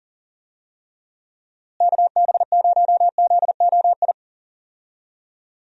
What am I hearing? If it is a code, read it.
Morse code, heard as a keyed tone, about 40 wpm, 700 Hz. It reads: KB0ZOI
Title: USA Call Signs - 6 Characters - Repeated 2x - 40wpm